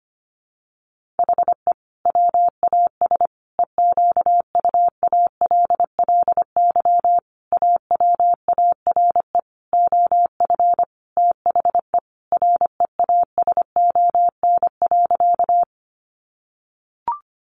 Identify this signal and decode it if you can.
Morse code, heard as a keyed tone, about 25 wpm, 700 Hz. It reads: HE WAS EQUALLY AWARE OF THE REASON.
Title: Sentences from Top 1400 Words 25wpm